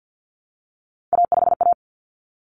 Morse code, heard as a keyed tone, about 50 wpm, 700 Hz. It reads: A5A
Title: Common Cut Numbers - 3-Digit 50wpm